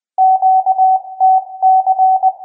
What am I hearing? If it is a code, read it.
Morse code, heard as a keyed tone, about 20 wpm, 760 Hz. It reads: QTC